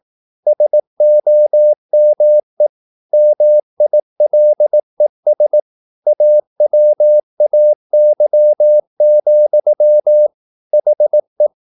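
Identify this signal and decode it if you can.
Morse code, heard as a keyed tone, about 18 wpm, 600 Hz. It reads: SOME MILES AWAY, HE
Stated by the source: Morse practice at 18 wpm, English text, from CW med Gnister.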